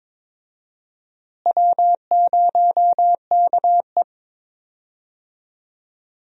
Morse code, heard as a keyed tone, about 22 wpm, 700 Hz. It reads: W0KE
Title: USA Call Signs - 4 Characters - Repeated 2x - 22wpm